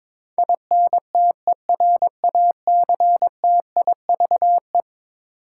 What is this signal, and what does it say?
Morse code, heard as a keyed tone, about 22 wpm, 700 Hz. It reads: INTERACTIVE
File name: practice / 22wpm / Eleven Letter Words - 22wpm